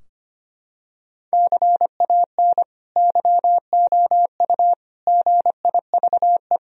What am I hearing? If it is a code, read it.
Morse code, heard as a keyed tone, about 25 wpm, 700 Hz. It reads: CAN YOU GIVE
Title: Sentences from Top 1000 Words - Review Part 2 - 25wpm